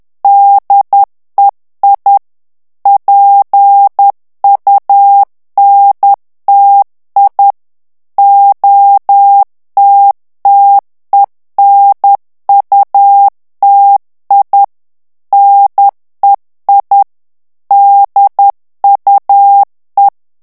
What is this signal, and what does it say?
Morse code, heard as a keyed tone, about 10 wpm, 800 Hz. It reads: DEI PUNTI OTTENUTI NEI DUE
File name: ITALIEN texte clair 2 vitesse 10 mots minute